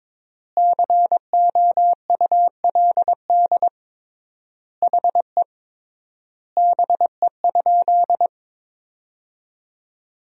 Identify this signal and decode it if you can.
Morse code, heard as a keyed tone, about 22 wpm, 700 Hz. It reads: COULD HE BE?
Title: 3x Word-Spacing - Sentences from Top 200 Words - 22wpm